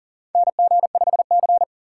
Morse code, heard as a keyed tone, about 40 wpm, 700 Hz. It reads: NG5C
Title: USA Call Signs - 4 Characters - 4x Character Spacing - 40wpm